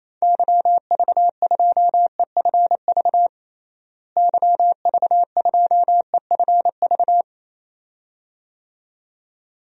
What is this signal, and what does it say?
Morse code, heard as a keyed tone, about 27 wpm, 700 Hz. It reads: YV2EFV YV2EFV
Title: DXCC-100 Call Signs - 6 Characters - Repeated 2x - 28wpm